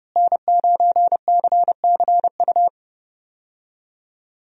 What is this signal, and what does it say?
Morse code, heard as a keyed tone, about 30 wpm, 700 Hz. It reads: N9CCU
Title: USA Call Signs - 5 Characters - Repeated 3x - 30wpm